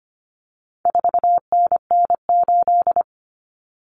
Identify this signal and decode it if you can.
Morse code, heard as a keyed tone, about 25 wpm, 700 Hz. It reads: 4NN8
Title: Common Cut Numbers - 4-Digit 25wpm